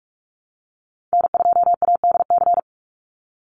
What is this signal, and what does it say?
Morse code, heard as a keyed tone, about 44 wpm, 700 Hz. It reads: N2ADC
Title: Call Signs Repeated 3x - 45wpm